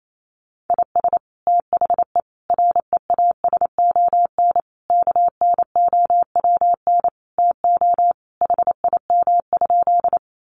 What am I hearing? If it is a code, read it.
Morse code, heard as a keyed tone, about 28 wpm, 700 Hz. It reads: IS THE REASON KNOWN TO HIM?
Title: Sentences from Top 700 Words 28wpm